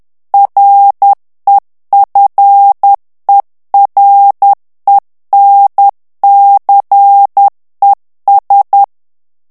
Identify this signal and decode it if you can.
Morse code, heard as a keyed tone, about 10 wpm, 800 Hz. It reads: REFERENCES
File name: ANGLAIS texte clair 3 vitesse 10 mots minute